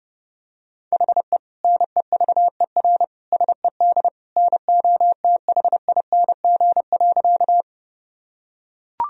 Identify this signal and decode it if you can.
Morse code, heard as a keyed tone, about 32 wpm, 700 Hz. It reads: HE NEVER SED NOTHING.
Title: Sentences from Top 500 Words and Abbreviations 30wpm